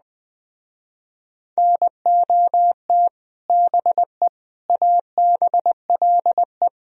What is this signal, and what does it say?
Morse code, heard as a keyed tone, about 21 wpm, 700 Hz. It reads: NOT BE ABLE